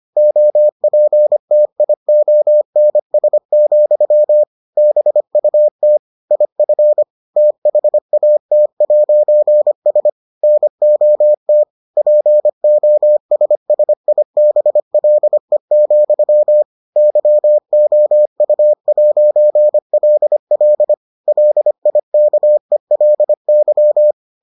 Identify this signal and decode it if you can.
Morse code, heard as a keyed tone, about 25 wpm, 600 Hz. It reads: OPTIONS, BUT IF THAT'S NOT POSSIBLE, YOU'LL LIKELY